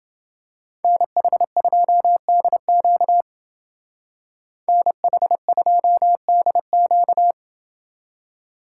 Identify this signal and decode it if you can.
Morse code, heard as a keyed tone, about 28 wpm, 700 Hz. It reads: NH2DQ NH2DQ